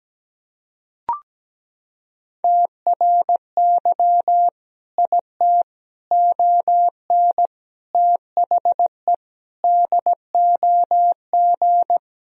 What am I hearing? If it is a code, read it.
Morse code, heard as a keyed tone, about 17 wpm, 700 Hz. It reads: TRY IT ON THE DOG
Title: Sentences from Top 1000 Words - Review Part 2 - 17wpm